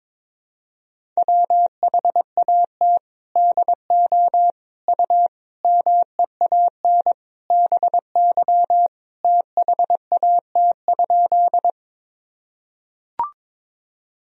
Morse code, heard as a keyed tone, about 22 wpm, 700 Hz. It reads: WHAT DO U MEAN BY THAT?
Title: Sentences from Top 200 Words and Abbreviations 22wpm